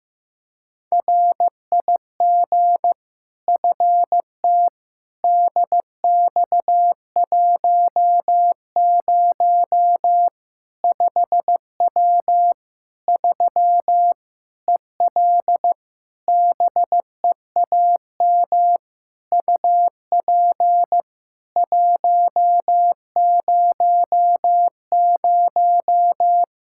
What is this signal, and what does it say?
Morse code, heard as a keyed tone, about 15 wpm, 700 Hz. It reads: RIG FT DX10 5W 3 EL BEAM UP 100
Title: Practice QSO - Exchange 2 - Info - 15wpm